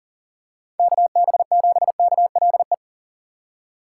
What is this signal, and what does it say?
Morse code, heard as a keyed tone, about 40 wpm, 700 Hz. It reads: KB7KLE
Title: USA Call Signs - 6 Characters - 4x Character Spacing - 40wpm